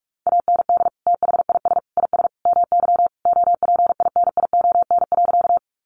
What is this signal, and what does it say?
Morse code, heard as a keyed tone, about 44 wpm, 700 Hz. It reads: AND THIS IS MY OPINION.
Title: Sentences from Top 1200 Words 45wpm